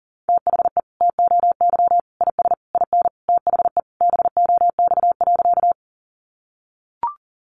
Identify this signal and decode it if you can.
Morse code, heard as a keyed tone, about 40 wpm, 700 Hz. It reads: THE TOY IS IN THE BOX.
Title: Easy Phrases 40wpm